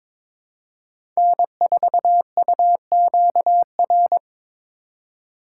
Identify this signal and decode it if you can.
Morse code, heard as a keyed tone, about 22 wpm, 700 Hz. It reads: N4UQR
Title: USA Call Signs - 5 Characters - 4x Character Spacing - 22wpm